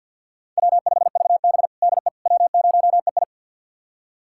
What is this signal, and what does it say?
Morse code, heard as a keyed tone, about 49 wpm, 700 Hz. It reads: W5VB DE W0EI
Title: USA Callsign DE Callsign - 4 Characters - 50wpm